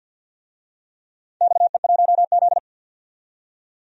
Morse code, heard as a keyed tone, about 47 wpm, 700 Hz. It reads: XE1Z
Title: DXCC-100 Call Signs - 4 Characters - 50wpm